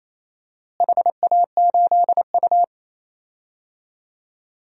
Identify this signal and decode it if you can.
Morse code, heard as a keyed tone, about 28 wpm, 700 Hz. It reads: HA8U
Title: DXCC-100 Call Signs - 4 Characters - Repeated 3x - 28wpm